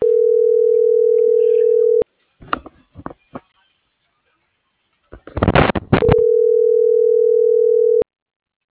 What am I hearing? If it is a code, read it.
Ringback tone (North America), 440 + 480 Hz.